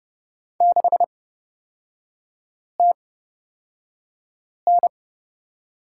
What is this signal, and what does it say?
Morse code, heard as a keyed tone, about 30 wpm, spaced as for 5 wpm, 700 Hz. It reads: 6TN
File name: Lesson 38e - Letter q - Three Characters 30wpm